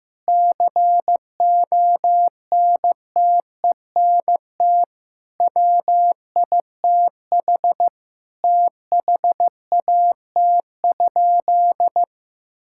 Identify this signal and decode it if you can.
Morse code, heard as a keyed tone, about 15 wpm, 700 Hz. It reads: CONTENT WITH THAT?
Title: Sentences from Top 1000 Words - Review Part 6 - 15wpm